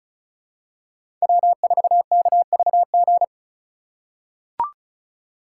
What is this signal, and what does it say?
Morse code, heard as a keyed tone, about 34 wpm, 700 Hz. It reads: W4KVG
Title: USA Call Signs - 2x Character Spacing - 35wpm